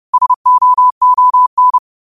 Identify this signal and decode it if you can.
Morse code, heard as a keyed tone, about 30 wpm, 1000 Hz. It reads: SOON
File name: qotdfiles / 30wpm